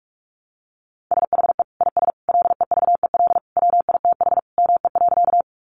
Morse code, heard as a keyed tone, about 44 wpm, 700 Hz. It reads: SHE IS LEVEL WITH ME.
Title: Sentences from Top 700 Words 45wpm